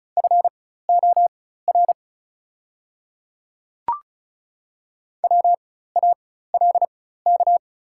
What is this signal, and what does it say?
Morse code, heard as a keyed tone, about 36 wpm, spaced as for 16 wpm, 690 Hz. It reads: FOR WALK
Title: 4x Character Spacing - Top 500 Words - 35wpm